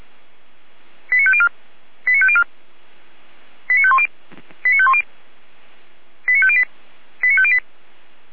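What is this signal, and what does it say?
Five-tone selective call (ZVEI), heard as quick runs of five tones, tones of about 70 ms. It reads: 89504 89504 89510 89510 89508 89508
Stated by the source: ZVEI five-tone selective call received by SDR.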